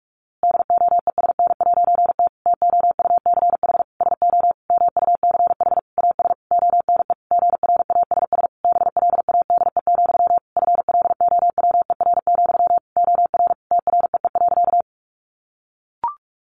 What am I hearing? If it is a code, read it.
Morse code, heard as a keyed tone, about 44 wpm, 700 Hz. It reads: DOESN'T TOUCH SO MUCH AS ONE GRASS BLADE, FLOWER, OR TREE.
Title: A Sound Of Thunder 45wpm